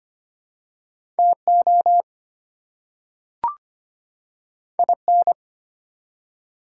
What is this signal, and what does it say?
Morse code, heard as a keyed tone, about 25 wpm, 700 Hz. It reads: TO IN
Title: Two Letter Words - 25wpm